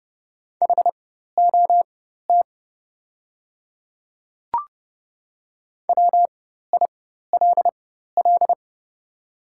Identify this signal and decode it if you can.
Morse code, heard as a keyed tone, about 30 wpm, spaced as for 14 wpm, 690 Hz. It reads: HOT WILL
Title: 4x Character Spacing - Top 100 Words - 30wpm